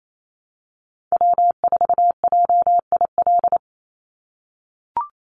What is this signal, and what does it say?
Morse code, heard as a keyed tone, about 28 wpm, 700 Hz. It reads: W4JIL
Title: USA Call Signs 28wpm